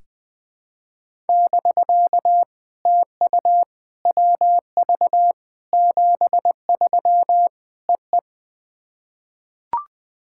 Morse code, heard as a keyed tone, about 20 wpm, 700 Hz. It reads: <BK> TU WV 73 EE
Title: POTA QSO Practice - 20wpm